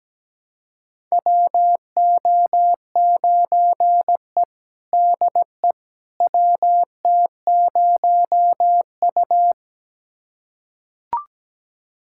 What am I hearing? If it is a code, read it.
Morse code, heard as a keyed tone, about 17 wpm, 700 Hz. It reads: WO9E DE WT0U
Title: USA Callsign DE Callsign - 4 Characters - 17wpm